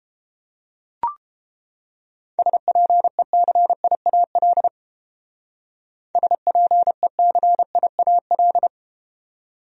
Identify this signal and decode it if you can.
Morse code, heard as a keyed tone, about 31 wpm, 700 Hz. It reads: SPECIAL SPECIAL